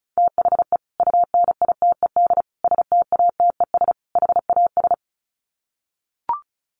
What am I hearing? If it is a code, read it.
Morse code, heard as a keyed tone, about 34 wpm, 700 Hz. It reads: THE UNITED STATES HAS
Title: Sets of 4 Words 35wpm